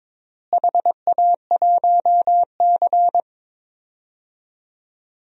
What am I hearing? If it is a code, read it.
Morse code, heard as a keyed tone, about 22 wpm, 700 Hz. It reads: HA1C
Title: DXCC-100 Call Signs - 4 Characters - Repeated 3x - 22wpm